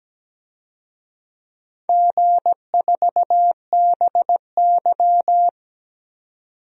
Morse code, heard as a keyed tone, about 17 wpm, 700 Hz. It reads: G4BY
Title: DXCC-100 Call Signs - 4 Characters - 17wpm